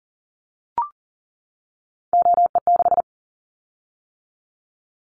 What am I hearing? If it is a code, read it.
Morse code, heard as a keyed tone, about 38 wpm, 690 Hz. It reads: OE6
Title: Callsign Prefixes for Top 100 DXCC - 40wpm